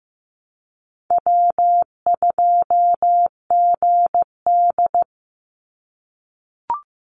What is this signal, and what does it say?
Morse code, heard as a keyed tone, about 15 wpm, 700 Hz. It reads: W2GD
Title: Easy Call Signs - 15wpm